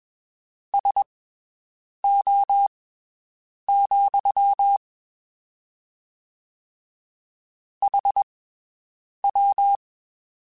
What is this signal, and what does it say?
Morse code, heard as a keyed tone, about 21 wpm, spaced as for 7 wpm, 800 Hz. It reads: SO, HW
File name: (spaced) / QSOs Spaced-mp3s / WD0FBT 1 @20 sp